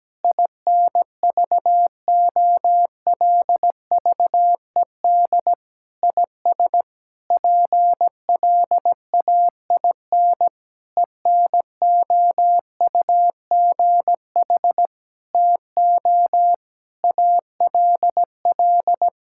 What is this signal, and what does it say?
Morse code, heard as a keyed tone, about 17 wpm, 700 Hz. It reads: INVOLVED IS PLAIN ENOUGH TO ALL
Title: Sentences from Top 1000 Words - Review Part 5 - 17wpm